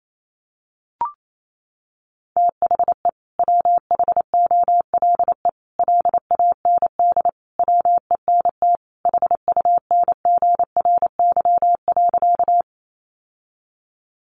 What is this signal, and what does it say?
Morse code, heard as a keyed tone, about 28 wpm, 700 Hz. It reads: THE WHOLE LAND WENT HUNGRY.
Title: Sentences from Top 1400 Words 28wpm